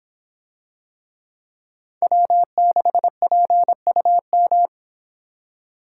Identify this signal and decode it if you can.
Morse code, heard as a keyed tone, about 25 wpm, 700 Hz. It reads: W6PUM